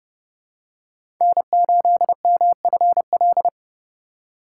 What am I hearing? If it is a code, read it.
Morse code, heard as a keyed tone, about 30 wpm, 700 Hz. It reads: N8MFL
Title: USA Call Signs - 5 Characters - 4x Character Spacing - 30wpm